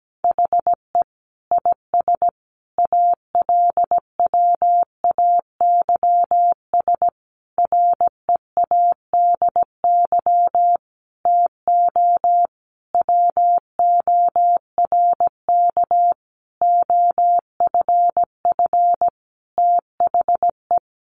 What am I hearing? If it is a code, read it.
Morse code, heard as a keyed tone, about 17 wpm, 700 Hz. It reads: HE IS ALWAYS READY TO WORK OFF THE
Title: Sentences from Top 1300 Words 17wpm